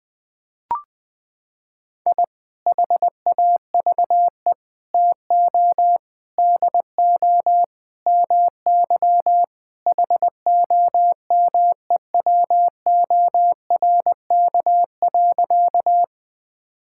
Morse code, heard as a keyed tone, about 21 wpm, 700 Hz. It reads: I HAVE TO DO MY HOMEWORK.
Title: Sentences with 3-Word Phrases - 20wpm